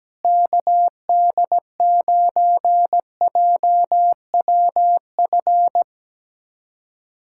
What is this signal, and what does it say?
Morse code, heard as a keyed tone, about 17 wpm, 700 Hz. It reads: KD9JWF
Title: USA Callsign DE Callsign - 6 Characters - 17wpm